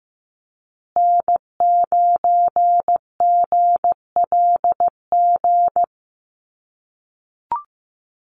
Morse code, heard as a keyed tone, about 15 wpm, 700 Hz. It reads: N9GLG
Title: USA Call Signs Repeated 2x 15wpm